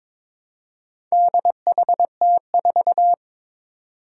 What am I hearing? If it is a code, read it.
Morse code, heard as a keyed tone, about 22 wpm, 700 Hz. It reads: DHT4
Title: Four Challenging Letter-Number Combinations 22wpm